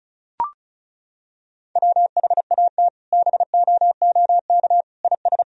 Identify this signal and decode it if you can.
Morse code, heard as a keyed tone, about 36 wpm, 700 Hz. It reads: WHAT BOOK IS
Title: Sentences from Top 1000 Words - Review Part 1 - 35wpm